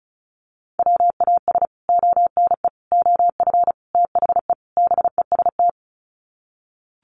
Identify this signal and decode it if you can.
Morse code, heard as a keyed tone, about 34 wpm, 700 Hz. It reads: WAS ONE OF THE BEST
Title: Sets of 5 Words Encore2 35wpm